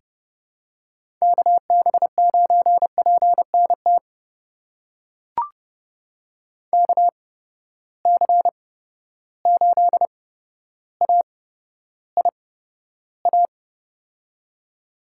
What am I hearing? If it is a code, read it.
Morse code, heard as a keyed tone, about 30 wpm, spaced as for 13 wpm, 700 Hz. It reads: KB9PNT KC8AIA
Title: USA Call Signs - 6 Characters - 8x Character Spacing - 30wpm